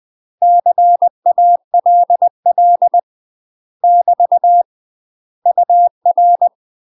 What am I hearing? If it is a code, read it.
Morse code, heard as a keyed tone, about 20 wpm, 700 Hz. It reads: CALL <BT> UR
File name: UK MASTER QSOs TRAINING / 20WPM / UK MASTER QSOs 0025